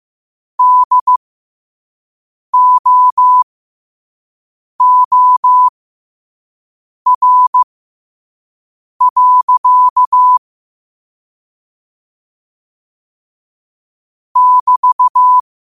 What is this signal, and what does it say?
Morse code, heard as a keyed tone, about 15 wpm, spaced as for 5 wpm, 1000 Hz. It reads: DOOR. <BT>
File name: qotdfiles / 5wpm